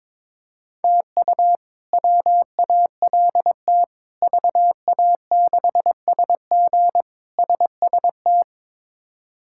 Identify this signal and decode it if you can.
Morse code, heard as a keyed tone, about 22 wpm, 700 Hz. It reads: TU WALT VA6SG SST
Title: K1USN SST QSO Practice - 22wpm